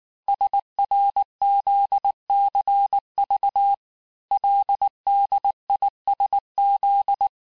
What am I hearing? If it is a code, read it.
Morse code, heard as a keyed tone, about 19 wpm, 800 Hz. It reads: SRZCV LDISZ
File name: cours-du-lundi files / cours-1583--texte-chiffre-18-mpm